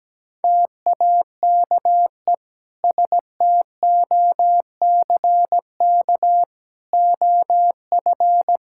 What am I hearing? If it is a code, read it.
Morse code, heard as a keyed tone, about 17 wpm, 700 Hz. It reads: TAKE STOCK OF
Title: Sentences from Top 1000 Words - Review Part 5 - 17wpm